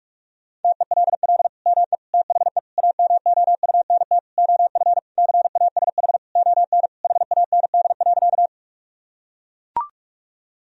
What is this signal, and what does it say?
Morse code, heard as a keyed tone, about 41 wpm, 700 Hz. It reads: TELL ME THE AMOUNT OF CASH ON HAND.
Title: Sentences from Top 1000 Words - Review Part 4 - 45wpm